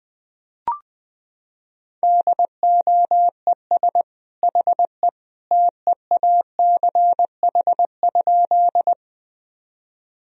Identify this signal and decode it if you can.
Morse code, heard as a keyed tone, about 21 wpm, 700 Hz. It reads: DOES HE TEACH?
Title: Sentences from Top 1000 Words - Review Part 2 - 20wpm